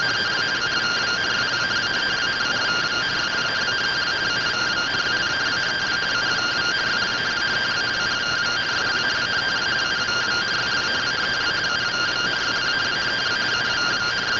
Two-tone teleprinter signal, about 170 Hz shift.